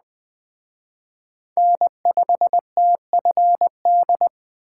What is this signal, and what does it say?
Morse code, heard as a keyed tone, about 20 wpm, 700 Hz. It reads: N5TFD
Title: Speed Racing - USA 1x3 Call Signs 17wpm